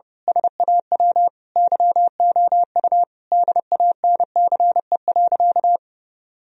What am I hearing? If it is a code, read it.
Morse code, heard as a keyed tone, about 29 wpm, 700 Hz. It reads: SAW YOU DANCE.
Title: Sentences from Top 1000 Words - Review Part 4 - 30wpm